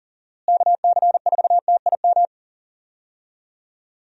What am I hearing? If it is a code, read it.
Morse code, heard as a keyed tone, about 40 wpm, 700 Hz. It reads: KC4TIM
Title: USA Call Signs - 6 Characters - Repeated 3x - 40wpm